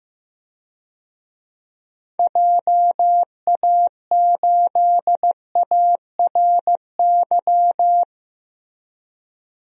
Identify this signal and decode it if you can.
Morse code, heard as a keyed tone, about 15 wpm, 700 Hz. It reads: JA8ARY
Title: DXCC-100 Call Signs - 6 Characters - Repeated 2x - 15wpm